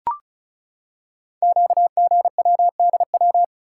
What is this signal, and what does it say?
Morse code, heard as a keyed tone, about 36 wpm, 700 Hz. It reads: QGWDW
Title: Code Groups - 5 Letters - 35wpm